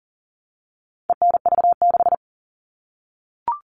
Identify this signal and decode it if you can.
Morse code, heard as a keyed tone, about 38 wpm, 700 Hz. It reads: ENV6
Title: All Cut Numbers - 4-Digit 40wpm